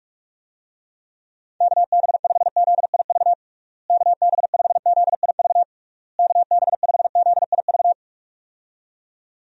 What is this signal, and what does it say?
Morse code, heard as a keyed tone, about 42 wpm, 700 Hz. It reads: KB5ZIV KB5ZIV KB5ZIV
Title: USA Call Signs - 6 Characters - Repeated 3x - 45wpm